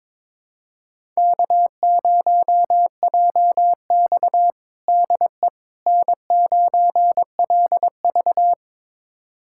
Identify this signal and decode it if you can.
Morse code, heard as a keyed tone, about 22 wpm, 700 Hz. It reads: K0JX DE N9LV
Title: USA Callsign DE Callsign - 4 Characters - 22wpm